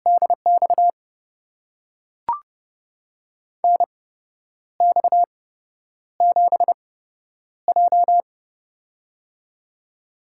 Morse code, heard as a keyed tone, about 32 wpm, spaced as for 10 wpm, 700 Hz. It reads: DX NX7J
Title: USA Call Signs - 8x Character Spacing - 30wpm